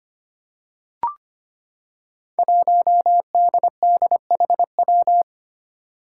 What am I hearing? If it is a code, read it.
Morse code, heard as a keyed tone, about 25 wpm, 700 Hz. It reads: 1DDHW
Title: Five Challenging Letter-Number Combinations 25wpm